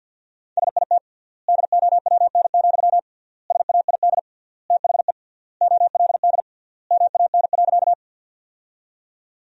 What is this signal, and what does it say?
Morse code, heard as a keyed tone, about 52 wpm, 700 Hz. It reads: SIT DOWN, SAID THE OLD MAN.
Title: 3x Word-Spacing - Sentences from Top 500 Words - 50wpm